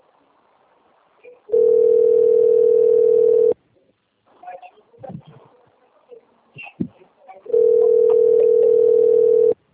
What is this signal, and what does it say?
Ringback tone (North America), 440 + 480 Hz.
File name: rg-1600-unknown- - - .13858